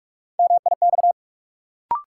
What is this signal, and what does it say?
Morse code, heard as a keyed tone, about 43 wpm, 700 Hz. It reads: MIX
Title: Sentences from Top 1100 Words - New Word Drill Alt - 45wpm